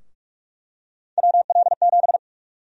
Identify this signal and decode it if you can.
Morse code, heard as a keyed tone, about 42 wpm, 700 Hz. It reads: WL7
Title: USA Call Sign Prefixes - 45wpm